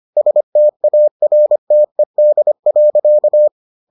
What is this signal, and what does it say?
Morse code, heard as a keyed tone, about 25 wpm, 600 Hz. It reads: STARTED.